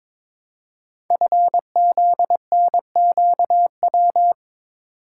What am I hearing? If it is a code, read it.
Morse code, heard as a keyed tone, about 23 wpm, 700 Hz. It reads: FZNQW
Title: Code Groups - 5 Letters - 22wpm